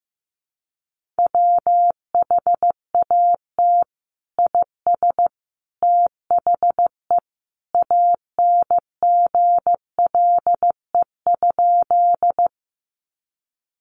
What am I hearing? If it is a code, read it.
Morse code, heard as a keyed tone, about 15 wpm, 700 Hz. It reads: WHAT IS THE ANGLE?
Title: Sentences from Top 1500 Words 15wpm